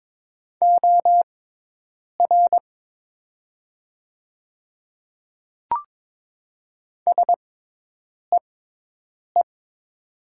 Morse code, heard as a keyed tone, about 22 wpm, spaced as for 8 wpm, 700 Hz. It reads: OR SEE
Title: 6x Character Spacing - Top 100 Words - 22wpm